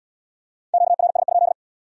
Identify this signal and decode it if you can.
Morse code, heard as a keyed tone, about 68 wpm, 700 Hz. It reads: QRS?